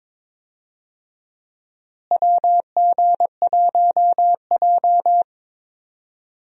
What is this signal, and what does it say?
Morse code, heard as a keyed tone, about 22 wpm, 700 Hz. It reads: WG1J
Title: USA Call Signs - 4 Characters - Repeated 3x - 22wpm